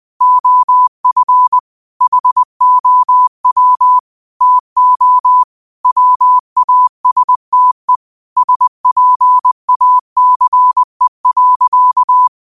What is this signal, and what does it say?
Morse code, heard as a keyed tone, about 20 wpm, 1000 Hz. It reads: OF HOW TO WASTE SPACE.
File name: qotdfiles / 20wpm